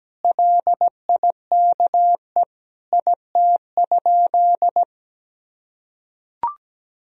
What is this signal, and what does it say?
Morse code, heard as a keyed tone, about 17 wpm, 700 Hz. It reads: LIKE IT?
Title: Sentences from Top 300 Words and Abbreviations 17wpm